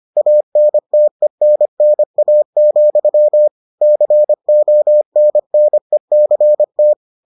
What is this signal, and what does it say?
Morse code, heard as a keyed tone, about 25 wpm, 600 Hz. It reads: ANTENNA, CONNECT